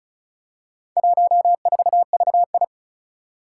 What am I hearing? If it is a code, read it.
Morse code, heard as a keyed tone, about 36 wpm, 700 Hz. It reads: 14VI
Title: Four Challenging Letter-Number Combinations 35wpm